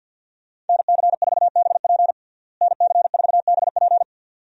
Morse code, heard as a keyed tone, about 49 wpm, 700 Hz. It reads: NC4BP NC4BP
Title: USA Call Signs - 5 Characters - Repeated 2x - 50wpm